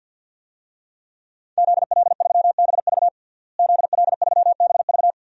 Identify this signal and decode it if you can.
Morse code, heard as a keyed tone, about 52 wpm, 700 Hz. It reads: ZL3BV ZL3BV
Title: DXCC-100 Call Signs - 5 Characters - Repeated 3x - 50wpm